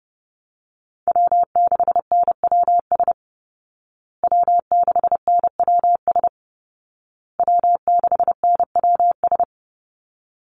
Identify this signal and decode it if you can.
Morse code, heard as a keyed tone, about 30 wpm, 700 Hz. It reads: W6NWS W6NWS W6NWS
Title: Call Signs Repeated 2x - 30wpm